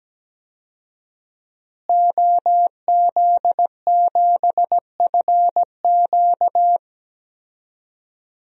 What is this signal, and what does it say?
Morse code, heard as a keyed tone, about 17 wpm, 700 Hz. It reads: OZ7FQ
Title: DXCC-100 Call Signs - 5 Characters - Repeated 2x - 17wpm